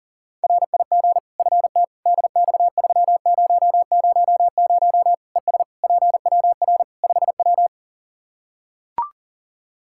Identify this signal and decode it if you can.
Morse code, heard as a keyed tone, about 40 wpm, 700 Hz. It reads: RIG FT DX3000 ES PWR 5W
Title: K1USN SST QSO Practice 40wpm